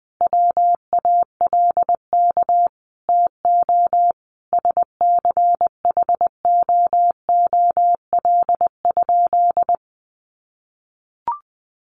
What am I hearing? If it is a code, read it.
Morse code, heard as a keyed tone, about 20 wpm, 700 Hz. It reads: WALK TO SCHOOL?
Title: Sentences from Top 1400 Words 20wpm